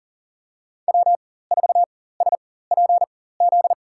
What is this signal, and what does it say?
Morse code, heard as a keyed tone, about 40 wpm, spaced as for 19 wpm, 700 Hz. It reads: W4SPZ
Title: USA Call Signs - 5 Characters - 4x Character Spacing - 40wpm